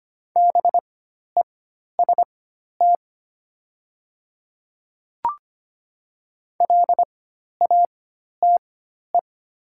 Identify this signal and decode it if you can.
Morse code, heard as a keyed tone, about 25 wpm, spaced as for 12 wpm, 700 Hz. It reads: BEST LATE
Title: 4x Character Spacing - Top 500 Words - 25wpm